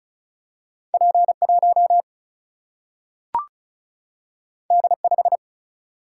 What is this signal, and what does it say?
Morse code, heard as a keyed tone, about 36 wpm, 700 Hz. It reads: P1 D5
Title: Single-Letter Single-Number 35wpm